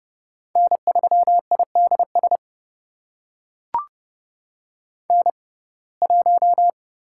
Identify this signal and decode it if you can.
Morse code, heard as a keyed tone, about 30 wpm, spaced as for 22 wpm, 700 Hz. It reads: N3IDS N1
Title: USA Call Signs - 5 Characters - 6x Character Spacing - 30wpm